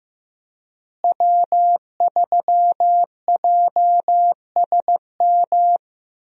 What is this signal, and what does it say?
Morse code, heard as a keyed tone, about 15 wpm, 700 Hz. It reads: W3JSM
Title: USA Call Signs - 5 Characters - Repeated 3x - 15wpm